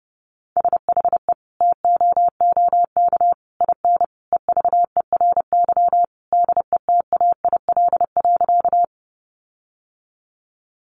Morse code, heard as a keyed tone, about 30 wpm, 700 Hz. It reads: SHE TOOK IN EVERY DETAIL.
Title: Sentences from Top 1500 Words 30wpm